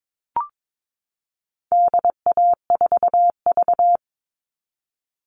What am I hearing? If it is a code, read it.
Morse code, heard as a keyed tone, about 22 wpm, 700 Hz. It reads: DA4V
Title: All Cut Numbers - 4-Digit 22wpm